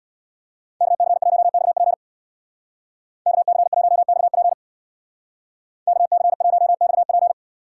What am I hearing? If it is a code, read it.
Morse code, heard as a keyed tone, about 62 wpm, 700 Hz. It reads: KC1XP KC1XP KC1XP
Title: Speed Racing - USA Call Signs 50wpm